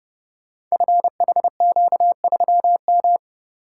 Morse code, heard as a keyed tone, about 30 wpm, 700 Hz. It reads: FHQ3M
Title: Code Groups - 5 Letter:Numbers - 30wpm